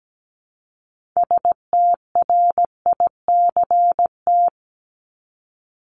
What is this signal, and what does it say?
Morse code, heard as a keyed tone, about 17 wpm, 700 Hz. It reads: STRICT
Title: Top 1501-1750 Words 17wpm